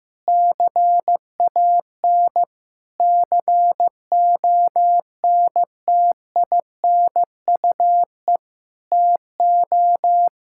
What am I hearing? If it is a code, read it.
Morse code, heard as a keyed tone, about 15 wpm, 700 Hz. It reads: CAN CONTINUE TO
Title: Sentences from Top 1000 Words - Review Part 6 - 15wpm